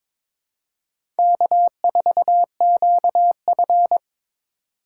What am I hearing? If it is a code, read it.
Morse code, heard as a keyed tone, about 22 wpm, 700 Hz. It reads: K4QF
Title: USA Call Signs - 4x Character Spacing - 22wpm